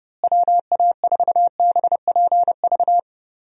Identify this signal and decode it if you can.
Morse code, heard as a keyed tone, about 30 wpm, 700 Hz. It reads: WA4BPV
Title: USA Call Signs - 6 Characters - Repeated 3x - 30wpm